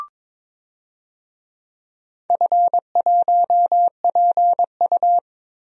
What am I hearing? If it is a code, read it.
Morse code, heard as a keyed tone, about 22 wpm, 700 Hz. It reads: F1PU
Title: DXCC-100 Call Signs - 4 Characters - Repeated 3x - 22wpm